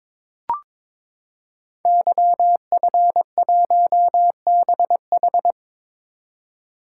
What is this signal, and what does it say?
Morse code, heard as a keyed tone, about 23 wpm, 700 Hz. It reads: YF1BH
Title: Code Groups - 5 Letter:Numbers - 22wpm